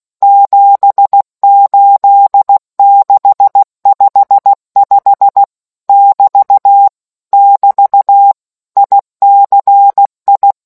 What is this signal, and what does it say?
Morse code, heard as a keyed tone, about 16 wpm, 800 Hz. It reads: 78655 <BT> <BT> ICI